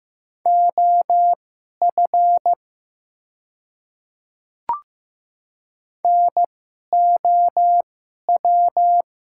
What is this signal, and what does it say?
Morse code, heard as a keyed tone, about 15 wpm, spaced as for 11 wpm, 700 Hz. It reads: OF NOW
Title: 2x Character Spacing - Top 100 Words - 15wpm